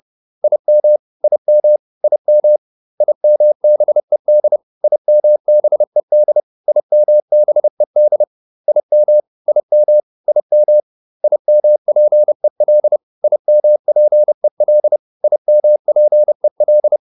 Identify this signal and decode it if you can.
Morse code, heard as a keyed tone, about 30 wpm, 600 Hz. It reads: IM IM IM IMBED IMBED IMBED IM IM IM IMPEL IMPEL IMPEL